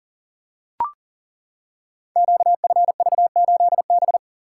Morse code, heard as a keyed tone, about 39 wpm, 700 Hz. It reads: QFV8B
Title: Code Groups - 5 Letter:Numbers - 40wpm